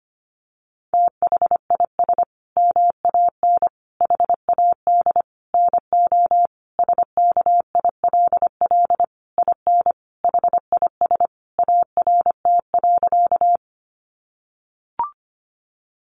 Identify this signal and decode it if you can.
Morse code, heard as a keyed tone, about 25 wpm, 700 Hz. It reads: THIS MAN HAD NO SKILL IN HIS ART.
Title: Sentences from Top 1200 Words 25wpm